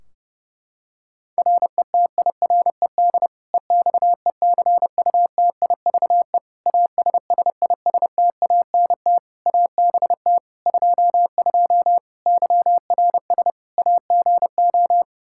Morse code, heard as a keyed tone, about 31 wpm, 700 Hz. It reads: RETIRED EXECUTIVE ASSISTANT ABT 22 YRS AGO
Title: Practice QSO - Exchange 3 - More Info - 30wpm